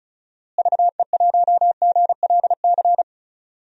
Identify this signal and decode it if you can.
Morse code, heard as a keyed tone, about 34 wpm, 700 Hz. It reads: VE1GLC
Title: DXCC-100 Call Signs - 6 Characters - Repeated 2x - 35wpm